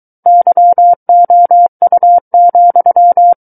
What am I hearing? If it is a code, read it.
Morse code, heard as a keyed tone, about 23 wpm, 700 Hz. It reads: YOU,
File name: Book One / 14-Chapter 14 - 23 WPM